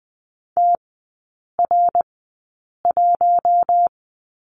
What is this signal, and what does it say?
Morse code, heard as a keyed tone, about 20 wpm, spaced as for 8 wpm, 700 Hz. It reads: TR1
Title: 060 Lesson 11e - Letter h - Three Characters 20wpm